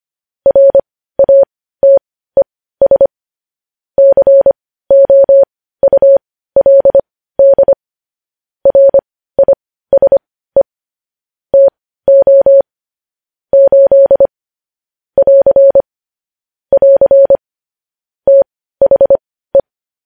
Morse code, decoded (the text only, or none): RATES COULD RISE TO 8 <AR> <AR> THE